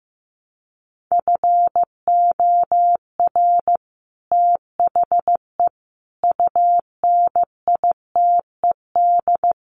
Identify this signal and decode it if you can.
Morse code, heard as a keyed tone, about 15 wpm, 700 Hz. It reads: FOR THE UNITED
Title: Sets of 4 Words 15wpm